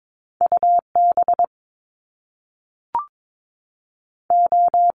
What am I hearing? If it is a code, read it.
Morse code, heard as a keyed tone, about 22 wpm, 700 Hz. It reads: UB O
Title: Top Two Letter Combinations in Words 22wpm